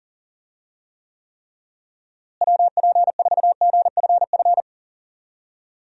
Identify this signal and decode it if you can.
Morse code, heard as a keyed tone, about 37 wpm, 700 Hz. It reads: WP4GFF
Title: USA Call Signs - 6 Characters - Repeated 3x - 40wpm